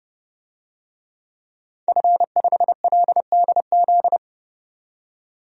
Morse code, heard as a keyed tone, about 30 wpm, 700 Hz. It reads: F5LDZ
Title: DXCC-100 Call Signs - 5 Characters - 30wpm